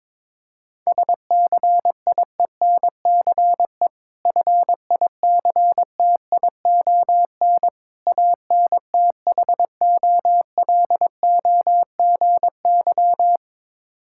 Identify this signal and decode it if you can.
Morse code, heard as a keyed tone, about 22 wpm, 690 Hz. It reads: SCIENCE FICTION ANTHOLOGY
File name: Difficult Words with Hints - 22wpm